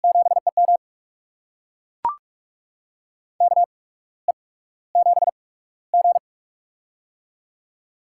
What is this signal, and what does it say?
Morse code, heard as a keyed tone, about 44 wpm, spaced as for 17 wpm, 700 Hz. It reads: 7EM KE7G
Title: USA Call Signs - 8x Character Spacing - 45wpm